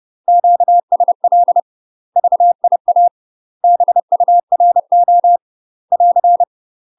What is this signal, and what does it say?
Morse code, heard as a keyed tone, about 30 wpm, 700 Hz. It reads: QSL VIA BURO <AR>
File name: UK MASTER QSOs TRAINING / 30WPM / UK MASTER QSOs 0044